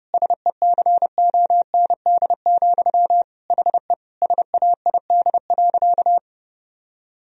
Morse code, heard as a keyed tone, about 32 wpm, 700 Hz. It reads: SECOND, HE SAID.
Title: Sentences from Top 1000 Words - Review Part 1 - 30wpm